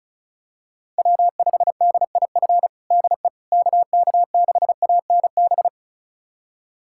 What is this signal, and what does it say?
Morse code, heard as a keyed tone, about 34 wpm, 700 Hz. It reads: W5DIF DE KK6ANB
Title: USA Callsign DE Callsign 35wpm